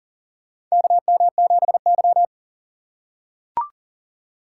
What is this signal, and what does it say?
Morse code, heard as a keyed tone, about 40 wpm, 700 Hz. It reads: KM7Y
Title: USA Call Signs - 4 Characters - 2x Character Spacing - 40wpm